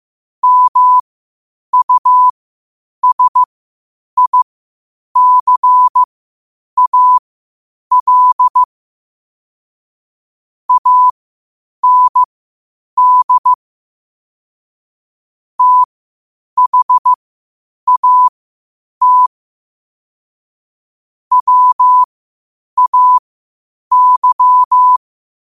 Morse code, heard as a keyed tone, about 15 wpm, spaced as for 8 wpm, 1000 Hz. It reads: MUSICAL AND THAT WAY
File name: qotdfiles / 8wpm